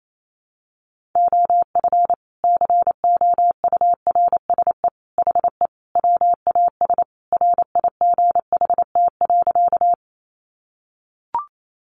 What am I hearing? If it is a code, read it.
Morse code, heard as a keyed tone, about 28 wpm, 700 Hz. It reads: OF COURSE HE WAS RIGHT.
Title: Sentences from Top 500 Words 28wpm